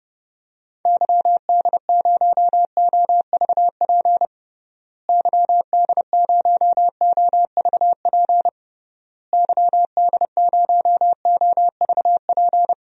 Morse code, heard as a keyed tone, about 30 wpm, 700 Hz. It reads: YD0OVP YD0OVP YD0OVP
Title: DXCC-100 Call Signs - 6 Characters - Repeated 3x - 30wpm